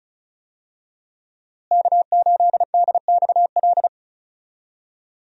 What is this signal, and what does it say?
Morse code, heard as a keyed tone, about 34 wpm, 700 Hz. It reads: K8DXL
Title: USA Call Signs - 5 Characters - Repeated 3x - 35wpm